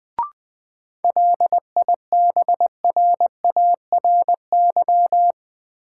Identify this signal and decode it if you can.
Morse code, heard as a keyed tone, about 21 wpm, 700 Hz. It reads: LIBRARY
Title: Sentences from Top 600 Words - New Word Drill - 20wpm